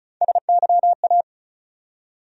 Morse code, heard as a keyed tone, about 34 wpm, 700 Hz. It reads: SYA
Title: USA Call Sign Postfixes - 3 Letters 35wpm